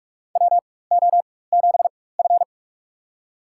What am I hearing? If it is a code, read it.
Morse code, heard as a keyed tone, about 45 wpm, spaced as for 21 wpm, 700 Hz. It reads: WO7F
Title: USA Call Signs - 4 Characters - 4x Character Spacing - 45wpm